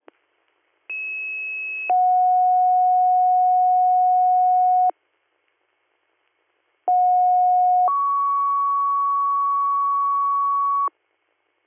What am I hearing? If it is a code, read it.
Two-tone page, heard as a one-second tone, then a three-second tone, about 2600 then 730 Hz, about 730 then 1120 Hz. It is heard from 0.9 to 10.9 s.